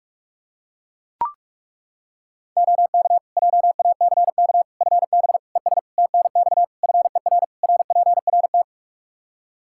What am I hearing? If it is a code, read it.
Morse code, heard as a keyed tone, about 42 wpm, 700 Hz. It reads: OK JACK FB ES TNX FER RPRT
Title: Practice QSO - Exchange 2 - Info - 45wpm